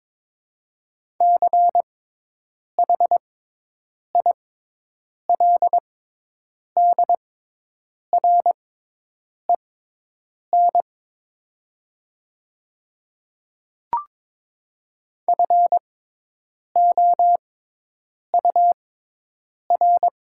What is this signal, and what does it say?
Morse code, heard as a keyed tone, about 22 wpm, spaced as for 8 wpm, 700 Hz. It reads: CHILDREN FOUR
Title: 6x Character Spacing - Top 500 Words - 22wpm